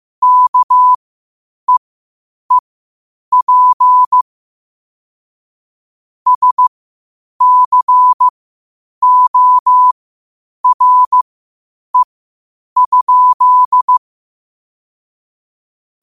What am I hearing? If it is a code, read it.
Morse code, heard as a keyed tone, about 15 wpm, spaced as for 8 wpm, 1000 Hz. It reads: KEEP SCORE?
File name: qotdfiles / 8wpm